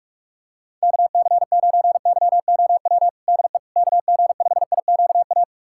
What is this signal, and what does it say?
Morse code, heard as a keyed tone, about 46 wpm, 700 Hz. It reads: KC9YOW DE KG5IQA
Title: USA Callsign DE Callsign - 6 Characters - 45wpm